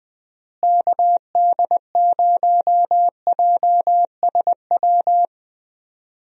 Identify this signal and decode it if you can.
Morse code, heard as a keyed tone, about 21 wpm, 700 Hz. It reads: KD0JSW
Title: USA Call Signs - 6 Characters - Repeated 3x - 20wpm